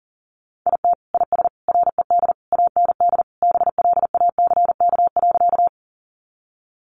Morse code, heard as a keyed tone, about 40 wpm, 700 Hz. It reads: IT IS RED AND BLACK.
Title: Easy Phrases 40wpm